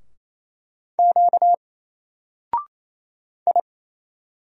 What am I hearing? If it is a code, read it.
Morse code, heard as a keyed tone, about 27 wpm, 700 Hz. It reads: Q I
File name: Single-Letter-Number-28wpm